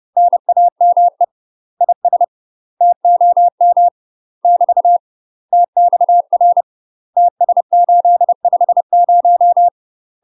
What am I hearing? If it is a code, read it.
Morse code, heard as a keyed tone, about 30 wpm, 700 Hz. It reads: NAME IS TOM <BT> TXR TS850
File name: UK MASTER QSOs TRAINING / 30WPM / UK MASTER QSOs 0046